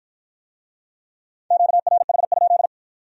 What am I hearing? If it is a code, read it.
Morse code, heard as a keyed tone, about 53 wpm, 700 Hz. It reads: QRH?